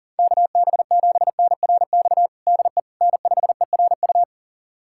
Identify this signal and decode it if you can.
Morse code, heard as a keyed tone, about 40 wpm, 700 Hz. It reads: KB7NRX DE N5ERU
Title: USA Callsign DE Callsign 40wpm